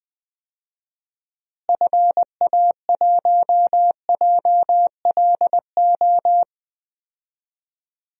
Morse code, heard as a keyed tone, about 20 wpm, 700 Hz. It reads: FA1JLO
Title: DXCC-100 Call Signs - 6 Characters - Repeated 2x - 20wpm